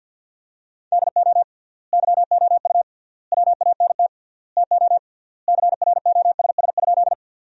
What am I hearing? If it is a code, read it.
Morse code, heard as a keyed tone, about 49 wpm, 700 Hz. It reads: DO YOU WANT TO CROSS?
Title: 3x Word-Spacing - Sentences from Top 300 Words - 50wpm